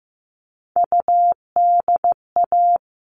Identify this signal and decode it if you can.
Morse code, heard as a keyed tone, about 15 wpm, 700 Hz. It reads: UDA
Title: All Cut Numbers - 3-Digit 15wpm